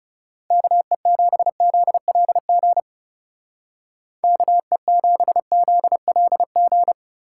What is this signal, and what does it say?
Morse code, heard as a keyed tone, about 36 wpm, 700 Hz. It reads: KE7ZLG KE7ZLG
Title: Speed Racing - USA Call Signs 30wpm